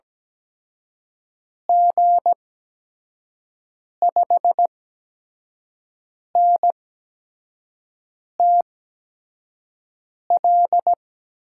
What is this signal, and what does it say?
Morse code, heard as a keyed tone, about 17 wpm, spaced as for 5 wpm, 700 Hz. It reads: G5NTL
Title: USA Call Signs - 6 Characters - 8x Character Spacing - 17wpm